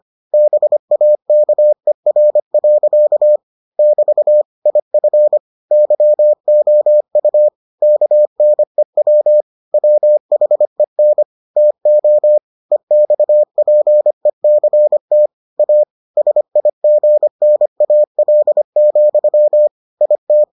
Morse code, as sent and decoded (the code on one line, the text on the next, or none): -... .- -.- . .-. .-.-.- / -...- / .. ..-. / -.-- --- ..- / -.- -. .-- / .-- .... -. / - --- / . -..- .--. -.-. - / .- / ... .. --. -. .- .-.. --..-- / .. -
BAKER. <BT> IF YOU KNEW WHEN TO EXPECT A SIGNAL, IT